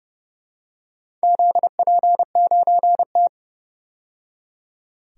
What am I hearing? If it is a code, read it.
Morse code, heard as a keyed tone, about 30 wpm, 700 Hz. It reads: ZP9T
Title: DXCC-100 Call Signs - 4 Characters - 30wpm